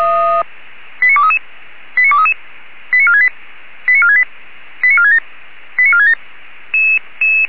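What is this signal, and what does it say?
Five-tone selective call (ZVEI), heard as quick runs of five tones, tones of about 70 ms. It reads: 89233 89233 89468 89468 89467 89467
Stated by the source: ZVEI five-tone selective call 89239 received on an SDR.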